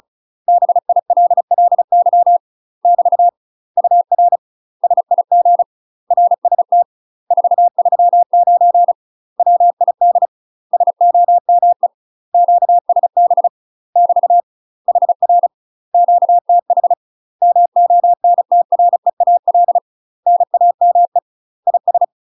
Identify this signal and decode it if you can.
Morse code, heard as a keyed tone, about 34 wpm, 700 Hz. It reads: BILLY <BT> UR SIG RST 439 WID SOME QSB <BT> HR QTH MONTREAL NAME IS
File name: UK MASTER QSOs TRAINING / 35WPM / UK MASTER QSOs 0042